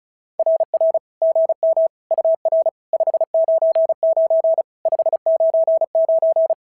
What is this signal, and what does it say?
Morse code, heard as a keyed tone, about 36 wpm, 650 Hz. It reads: RR GM UR 599 599
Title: SOTA QSO Practice 35wpm